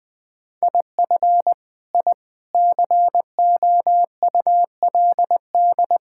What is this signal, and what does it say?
Morse code, heard as a keyed tone, about 21 wpm, 700 Hz. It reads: IF I COULD
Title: Sentences from Top 1000 Words - Review Part 6 - 20wpm